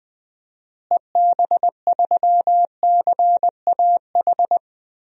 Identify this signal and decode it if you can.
Morse code, heard as a keyed tone, about 20 wpm, 700 Hz. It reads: EB3CAH
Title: DXCC-100 Call Signs - 6 Characters - Repeated 2x - 20wpm